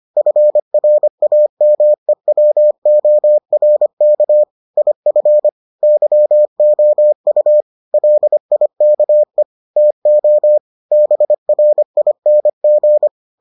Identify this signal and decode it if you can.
Morse code, heard as a keyed tone, about 25 wpm, 600 Hz. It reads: FRAMEWORK IF YOU LIKE TO BRING